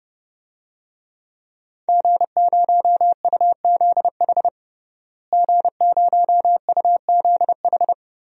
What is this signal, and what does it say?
Morse code, heard as a keyed tone, about 30 wpm, 700 Hz. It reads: G0UZH G0UZH
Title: DXCC-100 Call Signs - 5 Characters - Repeated 3x - 30wpm